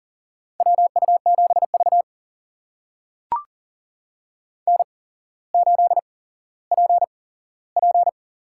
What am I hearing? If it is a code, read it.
Morse code, heard as a keyed tone, about 38 wpm, spaced as for 17 wpm, 700 Hz. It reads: WU7V N8PP
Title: USA Call Signs - 4 Characters - 8x Character Spacing - 40wpm